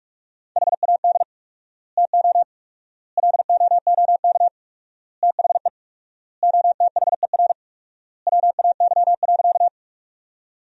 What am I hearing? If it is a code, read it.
Morse code, heard as a keyed tone, about 46 wpm, 700 Hz. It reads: HAD TO LOOK THE OTHER WAY.
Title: 4x Word-Spacing - Sentences from Top 100 Words - 45wpm